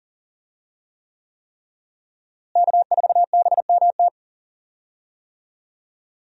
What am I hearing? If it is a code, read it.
Morse code, heard as a keyed tone, about 40 wpm, 700 Hz. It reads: K4BMT
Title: USA Call Signs - 5 Characters - Repeated 3x - 40wpm